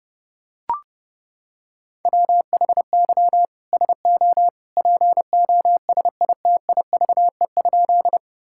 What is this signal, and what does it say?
Morse code, heard as a keyed tone, about 30 wpm, 700 Hz. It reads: WHY SO POSITIVE?
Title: Sentences from Top 1000 Words - Review Part 5 - 30wpm